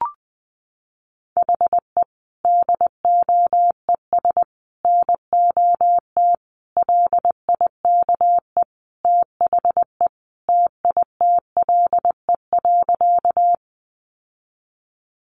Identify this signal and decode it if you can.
Morse code, heard as a keyed tone, about 20 wpm, 700 Hz. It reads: HE DOES NOT LIKE THE TITLE.
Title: Sentences from Top 1200 Words 20wpm